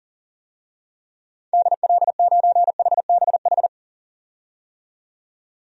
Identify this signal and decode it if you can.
Morse code, heard as a keyed tone, about 40 wpm, 700 Hz. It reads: DL9HBH
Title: DXCC-100 Call Signs - 6 Characters - 40wpm